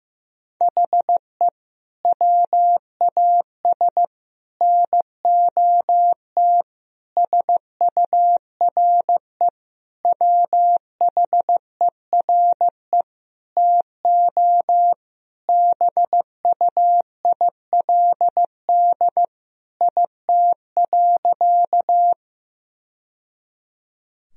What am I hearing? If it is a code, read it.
Morse code, heard as a keyed tone, about 15 wpm, 700 Hz. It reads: HE WAS NOT SURE WHERE TO BUILD IT.
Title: Sentences from Top 1000 Words - Review Part 1 - 15wpm